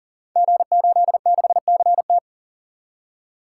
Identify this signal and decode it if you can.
Morse code, heard as a keyed tone, about 42 wpm, 700 Hz. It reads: G86CT
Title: Code Groups - 5 Letter:Numbers - 40wpm